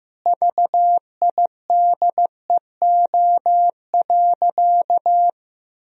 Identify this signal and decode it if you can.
Morse code, heard as a keyed tone, about 15 wpm, 700 Hz. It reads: VIDEO.
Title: Sentences from Top 1000 Words - Review Part 4 - 15wpm